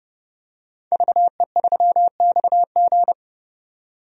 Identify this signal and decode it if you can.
Morse code, heard as a keyed tone, about 30 wpm, 700 Hz. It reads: VE3XG
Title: DXCC-100 Call Signs - 5 Characters - 30wpm